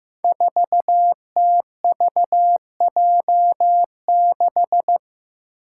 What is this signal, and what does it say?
Morse code, heard as a keyed tone, about 15 wpm, 700 Hz. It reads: 4TVJ6
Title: Five Challenging Letter-Number Combinations 15wpm